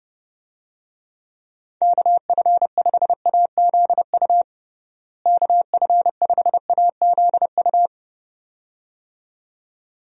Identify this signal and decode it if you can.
Morse code, heard as a keyed tone, about 30 wpm, 700 Hz. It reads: KF5AZU KF5AZU
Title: USA Call Signs - 6 Characters - Repeated 2x - 30wpm